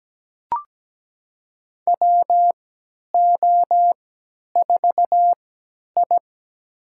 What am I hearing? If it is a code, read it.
Morse code, heard as a keyed tone, about 17 wpm, spaced as for 10 wpm, 700 Hz. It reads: WO4I
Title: USA Call Signs - 4 Characters - 3x Character Spacing - 17wpm